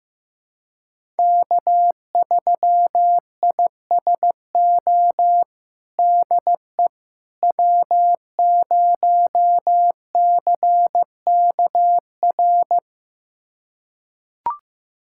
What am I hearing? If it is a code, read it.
Morse code, heard as a keyed tone, about 15 wpm, 700 Hz. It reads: K3ISO DE W0CKR
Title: USA Callsign DE Callsign - 5 Characters - 15wpm